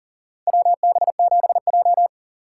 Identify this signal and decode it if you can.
Morse code, heard as a keyed tone, about 40 wpm, 700 Hz. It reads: WB7J
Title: USA Call Signs - 4 Characters - 2x Character Spacing - 40wpm